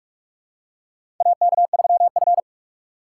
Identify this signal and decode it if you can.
Morse code, heard as a keyed tone, about 42 wpm, 700 Hz. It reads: AK3F
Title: USA Call Signs - 4 Characters - 2x Character Spacing - 45wpm